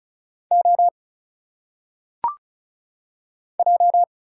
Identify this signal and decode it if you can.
Morse code, heard as a keyed tone, about 34 wpm, 700 Hz. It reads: O J
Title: Single Letters - Phonetic Pronunciation 35wpm